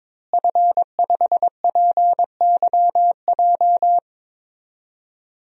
Morse code, heard as a keyed tone, about 22 wpm, 700 Hz. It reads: F5PYJ
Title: DXCC-100 Call Signs - 5 Characters - Repeated 2x - 22wpm